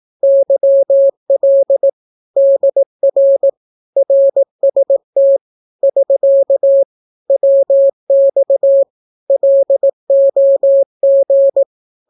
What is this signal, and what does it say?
Morse code, heard as a keyed tone, about 18 wpm, 550 Hz. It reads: YL DR RST <SK> WX LOG
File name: naukapodstaw / skroty / skroty